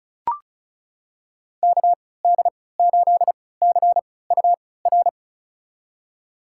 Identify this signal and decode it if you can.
Morse code, heard as a keyed tone, about 36 wpm, spaced as for 20 wpm, 700 Hz. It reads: KD8CUR
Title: USA Call Signs - 6 Characters - 3x Character Spacing - 35wpm